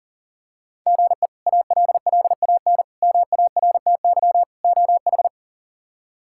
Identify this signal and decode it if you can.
Morse code, heard as a keyed tone, about 37 wpm, 700 Hz. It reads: GE ALLAN MARTY OH
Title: K1USN SST QSO Practice - 40wpm